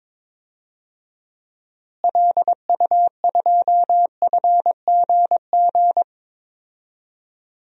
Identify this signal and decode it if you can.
Morse code, heard as a keyed tone, about 22 wpm, 700 Hz. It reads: LU2FGG
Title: DXCC-100 Call Signs - 6 Characters - Repeated 2x - 22wpm